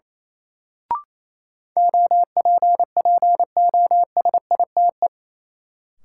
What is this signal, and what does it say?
Morse code, heard as a keyed tone, about 28 wpm, 700 Hz. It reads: OPPOSITE
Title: Sets of 5 Words Encore - New Word Drill - 28wpm